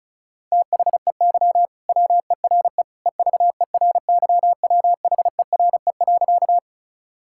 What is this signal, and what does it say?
Morse code, heard as a keyed tone, about 34 wpm, 700 Hz. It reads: THEY WERE EVERYWHERE.
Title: Sentences from Top 1600 Words 35wpm